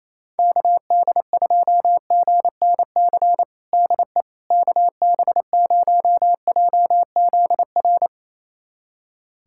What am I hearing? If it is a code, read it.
Morse code, heard as a keyed tone, about 28 wpm, 700 Hz. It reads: KD2GNC DE KB0JZR
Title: USA Callsign DE Callsign - 6 Characters - 28wpm